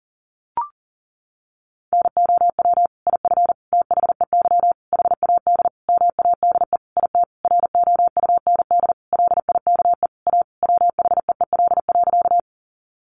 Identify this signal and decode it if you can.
Morse code, heard as a keyed tone, about 40 wpm, 700 Hz. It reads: NOW IF THEY HAD MADE IT ROUND LIKE A WHEEL.
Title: Sentences from Top 500 Words 40wpm